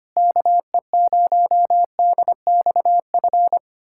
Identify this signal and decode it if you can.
Morse code, heard as a keyed tone, about 25 wpm, 700 Hz. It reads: KE0DXF
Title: USA Callsign DE Callsign - 6 Characters - 25wpm